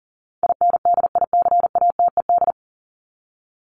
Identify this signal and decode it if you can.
Morse code, heard as a keyed tone, about 40 wpm, 700 Hz. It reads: INDICATED
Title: ED Suffix - Top 100 Words with Base Words 40wpm